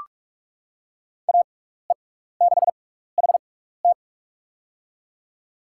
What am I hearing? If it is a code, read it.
Morse code, heard as a keyed tone, about 46 wpm, spaced as for 16 wpm, 700 Hz. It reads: AE6HT
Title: USA Call Signs - 6x Character Spacing - 45wpm